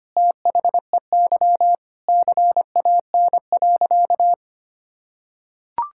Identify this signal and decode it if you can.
Morse code, heard as a keyed tone, about 25 wpm, 700 Hz. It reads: THEY CAN.
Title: Sentences from Top 1000 Words - Review Part 5 - 25wpm